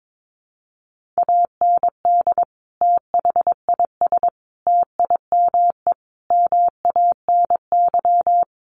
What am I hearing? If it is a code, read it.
Morse code, heard as a keyed tone, about 22 wpm, 700 Hz. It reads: AND THIS TIME MANY
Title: Sentences from Top 1400 Words 22wpm